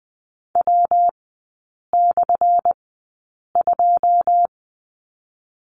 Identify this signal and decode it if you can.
Morse code, heard as a keyed tone, about 20 wpm, spaced as for 8 wpm, 700 Hz. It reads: W/2
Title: 178 Lesson 29e - Slash - Three Characters 20wpm